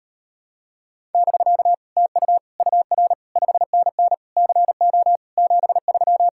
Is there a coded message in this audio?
Yes: Morse code from 1.1 to 6.3 s, heard as a keyed tone, about 38 wpm, 700 Hz. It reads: <BK> TU UR 5NN CO 73